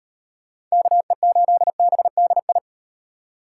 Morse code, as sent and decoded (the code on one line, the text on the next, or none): -.- . ---.. -... -.. ..
KE8BDI